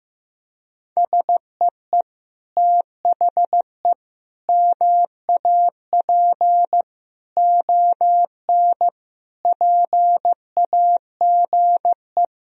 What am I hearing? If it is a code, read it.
Morse code, heard as a keyed tone, about 15 wpm, 700 Hz. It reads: SEE THE MAP ON PAGE
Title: Sentences from Top 1000 Words - Review Part 2 - 15wpm